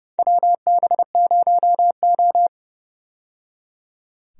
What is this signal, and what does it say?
Morse code, heard as a keyed tone, about 30 wpm, 700 Hz. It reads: WB0O
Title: USA Call Signs - 4 Characters - 30wpm